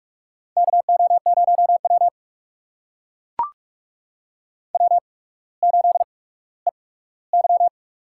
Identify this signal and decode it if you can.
Morse code, heard as a keyed tone, about 44 wpm, spaced as for 19 wpm, 700 Hz. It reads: KO0W W8EY
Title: USA Call Signs - 4 Characters - 8x Character Spacing - 45wpm